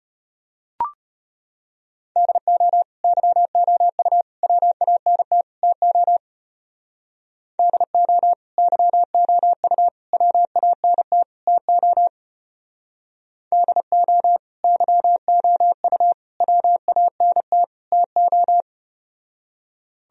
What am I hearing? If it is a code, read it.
Morse code, heard as a keyed tone, about 33 wpm, 700 Hz. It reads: DO YOU WANT TO DO YOU WANT TO DO YOU WANT TO